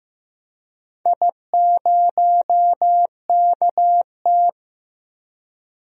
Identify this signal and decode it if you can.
Morse code, heard as a keyed tone, about 15 wpm, 700 Hz. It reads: I0KT
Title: DXCC-100 Call Signs - 4 Characters - Repeated 3x - 15wpm